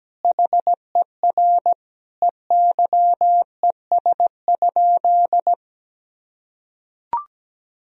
Morse code, heard as a keyed tone, about 17 wpm, 700 Hz. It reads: HER EYES?
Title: Sentences from Top 1000 Words - Review Part 4 - 17wpm